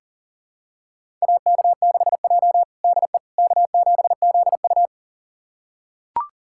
Morse code, heard as a keyed tone, about 40 wpm, 700 Hz. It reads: AK6J DE K7ZU
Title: USA Callsign DE Callsign - 4 Characters - 40wpm